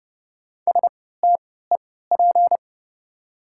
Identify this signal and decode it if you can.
Morse code, heard as a keyed tone, about 30 wpm, spaced as for 17 wpm, 700 Hz. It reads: STEP
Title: 3x Character Spacing - Top 500 Words - 30wpm